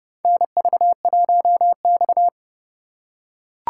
Morse code, heard as a keyed tone, about 30 wpm, 700 Hz. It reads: NV1X
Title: USA Call Signs - 4 Characters - 3x Character Spacing - 30wpm